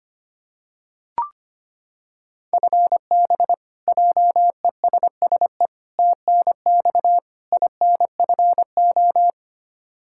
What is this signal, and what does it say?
Morse code, heard as a keyed tone, about 25 wpm, 700 Hz. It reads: FB JESSE TNX INFO
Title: Practice QSO - Exchange 3 - More Info - 25wpm